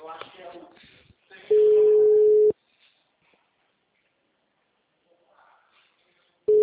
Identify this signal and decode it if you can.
Ringback tone (Europe, 425 Hz), with speech.